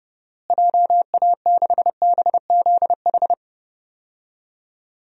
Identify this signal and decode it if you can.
Morse code, heard as a keyed tone, about 30 wpm, 700 Hz. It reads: JA6BZH
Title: DXCC-100 Call Signs - 6 Characters - Repeated 2x - 30wpm